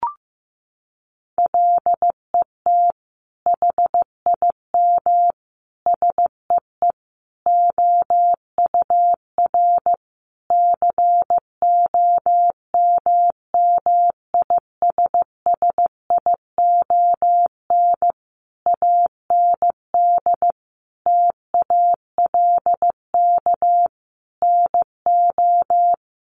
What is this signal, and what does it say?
Morse code, heard as a keyed tone, about 15 wpm, 700 Hz. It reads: LET HIM SEE OUR COMMISSION AND TALK NO
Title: Sentences from Top 1400 Words 15wpm